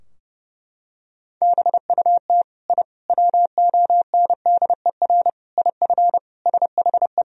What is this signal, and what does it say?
Morse code, heard as a keyed tone, about 30 wpm, 700 Hz. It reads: BUT I WONDER IF SHE
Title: Sentences from Top 1000 Words - Review Part 3 - 30wpm